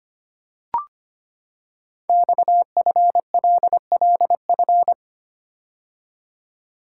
Morse code, heard as a keyed tone, about 25 wpm, 700 Hz. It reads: XFLLF
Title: Code Groups - 5 Letters - 25wpm